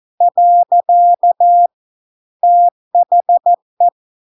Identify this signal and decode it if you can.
Morse code, heard as a keyed tone, about 14 wpm, 700 Hz. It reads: . THE